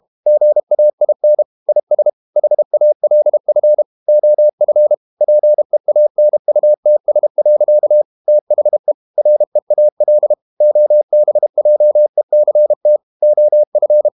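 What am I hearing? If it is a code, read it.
Morse code, heard as a keyed tone, about 32 wpm, 600 Hz. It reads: GAIN IS HALF OF PEANUTS. THE REAL OBJECT OF
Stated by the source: Morse code practice at 32 wpm from CW med Gnister, reading Ken Follett's novel Never in English.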